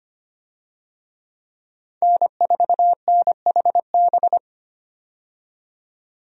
Morse code, heard as a keyed tone, about 25 wpm, 690 Hz. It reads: N4NHB
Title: USA Call Signs - 5 Characters - 25wpm